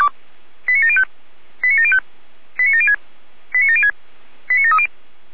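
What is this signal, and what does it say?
Five-tone selective call (ZVEI), heard as quick runs of five tones, tones of about 80 ms. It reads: 89705 89705 89706 89706 89730
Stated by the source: ZVEI five-tone selective call 89750 received on SDR.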